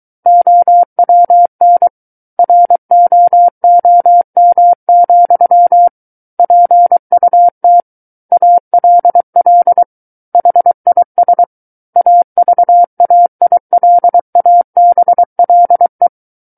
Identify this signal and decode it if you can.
Morse code, heard as a keyed tone, about 23 wpm, 700 Hz. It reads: OWN ROOM, PUT ALL HIS AVAILABLE
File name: Book One / 14-Chapter 14 - 23 WPM